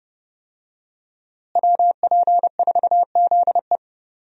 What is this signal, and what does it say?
Morse code, heard as a keyed tone, about 30 wpm, 700 Hz. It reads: WP4ZE
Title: USA Call Signs - 5 Characters - Repeated 3x - 30wpm